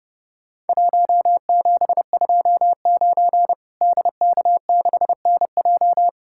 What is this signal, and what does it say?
Morse code, heard as a keyed tone, about 30 wpm, 700 Hz. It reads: 1729 DK6NJ
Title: Worked All Europe DX Contest - QTC Exchange - 30wpm